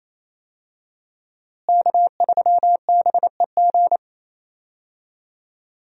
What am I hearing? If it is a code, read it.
Morse code, heard as a keyed tone, about 28 wpm, 700 Hz. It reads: K3BEG
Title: USA Call Signs - 5 Characters - Repeated 2x - 28wpm